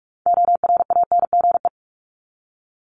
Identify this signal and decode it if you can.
Morse code, heard as a keyed tone, about 44 wpm, 700 Hz. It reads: ORANGE
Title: Top 1751-2000 Words 45wpm